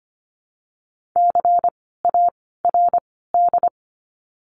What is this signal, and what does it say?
Morse code, heard as a keyed tone, about 25 wpm, 700 Hz. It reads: C A R D
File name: course / 25:10wpm / 094 Lesson 16f - Letter c - Words 25wpm